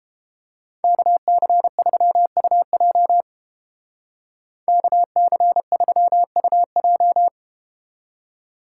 Morse code, heard as a keyed tone, about 30 wpm, 700 Hz. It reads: KC3UJ KC3UJ